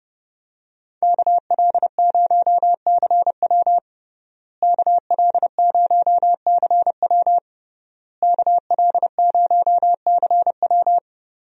Morse code, heard as a keyed tone, about 31 wpm, 700 Hz. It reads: KL0CW KL0CW KL0CW
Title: USA Call Signs - 5 Characters - Repeated 3x - 30wpm